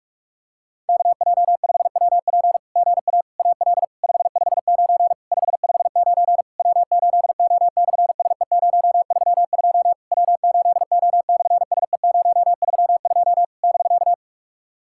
Morse code, heard as a keyed tone, about 43 wpm, 690 Hz. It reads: KJ5WP GA AL 559 559 W8O/SE032 W8O/SE032 <BK>
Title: SOTA QSO Practice 45wpm